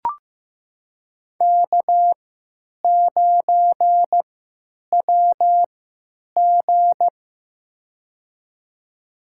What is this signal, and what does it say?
Morse code, heard as a keyed tone, about 15 wpm, spaced as for 9 wpm, 700 Hz. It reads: K9WG
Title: USA Call Signs - 4 Characters - 3x Character Spacing - 15wpm